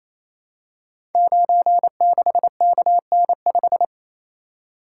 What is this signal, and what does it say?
Morse code, heard as a keyed tone, about 27 wpm, 700 Hz. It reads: 96KN5
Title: Code Groups - 5 Letter:Numbers - 28wpm